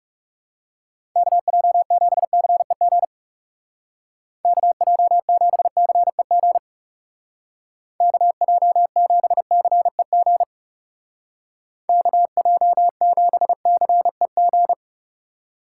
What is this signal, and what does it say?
Morse code, heard as a keyed tone, about 40 wpm, 700 Hz. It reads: KJ7CEG KJ7CEG KJ7CEG KJ7CEG